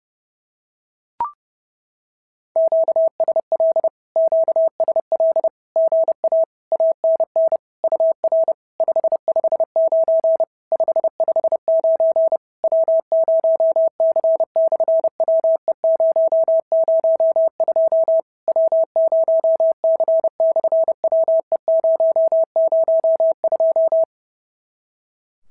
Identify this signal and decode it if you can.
Morse code, heard as a keyed tone, about 30 wpm, 650 Hz. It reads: QSL QSL GA ANN UR 559 559 W0C/WE002 W0C/WE002
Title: SOTA QSO Practice 30wpm